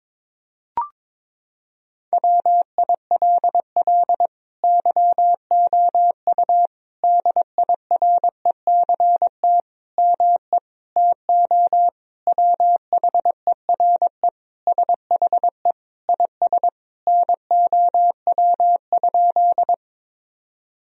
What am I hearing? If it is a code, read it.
Morse code, heard as a keyed tone, about 22 wpm, 700 Hz. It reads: WILL YOU DIRECT ME TO WHERE SHE IS NOW?
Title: Sentences from Top 1000 Words - Review Part 2 - 22wpm